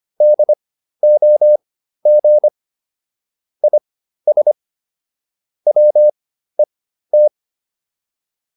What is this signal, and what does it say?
Morse code, heard as a keyed tone, about 25 wpm, spaced as for 13 wpm, 600 Hz. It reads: DOG IS WET
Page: intermediate-practice-files